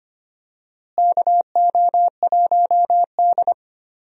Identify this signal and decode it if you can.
Morse code, heard as a keyed tone, about 25 wpm, 700 Hz. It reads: KO1D
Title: USA Call Signs - 4 Characters - Repeated 3x - 25wpm